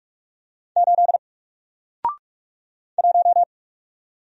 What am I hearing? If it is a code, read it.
Morse code, heard as a keyed tone, about 46 wpm, 700 Hz. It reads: 8 1
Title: Single Letter-Number Alt 45wpm